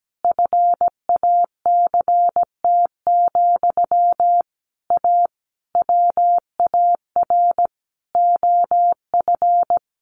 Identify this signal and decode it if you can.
Morse code, heard as a keyed tone, about 17 wpm, 700 Hz. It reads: FACT, A WAR OF
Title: Sentences from Top 1400 Words 17wpm